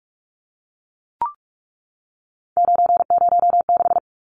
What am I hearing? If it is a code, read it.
Morse code, heard as a keyed tone, about 44 wpm, 700 Hz. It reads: 906
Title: Numbers - 3-Digit 45wpm